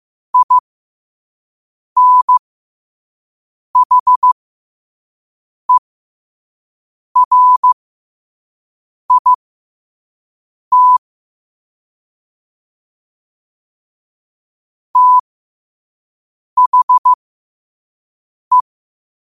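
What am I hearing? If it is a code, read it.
Morse code, heard as a keyed tone, about 15 wpm, spaced as for 5 wpm, 1000 Hz. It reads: INHERIT THE
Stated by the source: Morse code quote of the day at 5 wpm.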